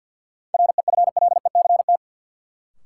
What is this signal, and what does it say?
Morse code, heard as a keyed tone, about 52 wpm, 700 Hz. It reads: REFLECT
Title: Sentences from Top 1500 Words - New Word Drill - 50wpm